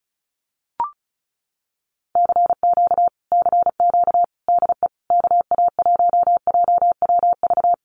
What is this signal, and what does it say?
Morse code, heard as a keyed tone, about 34 wpm, 700 Hz. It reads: CQ CQ DE KA1JWV
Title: Common QSO Phrases Encore 35wpm